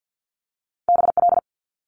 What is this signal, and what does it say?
Morse code, heard as a keyed tone, about 50 wpm, 700 Hz. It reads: BL
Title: Multi-letter Phonemes Quick Fire 50wpm